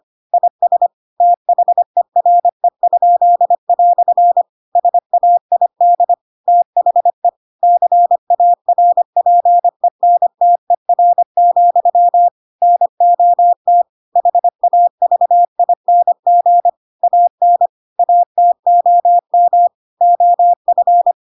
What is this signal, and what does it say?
Morse code, heard as a keyed tone, about 25 wpm, 700 Hz. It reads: IS THERE?" SAID THE CARPENTER, NOT HAVING AN ATOM OF